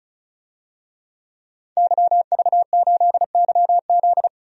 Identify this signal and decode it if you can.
Morse code, heard as a keyed tone, about 36 wpm, 700 Hz. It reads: YV8YZ
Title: DXCC-100 Call Signs - 5 Characters - 35wpm